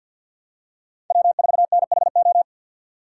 Morse code, heard as a keyed tone, about 52 wpm, 700 Hz. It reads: W4NHO
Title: USA Call Signs - 5 Characters - 2x Character Spacing - 50wpm